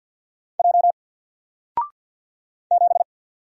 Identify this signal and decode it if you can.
Morse code, heard as a keyed tone, about 52 wpm, 700 Hz. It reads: J 7
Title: Single Letter-Number Alt 50wpm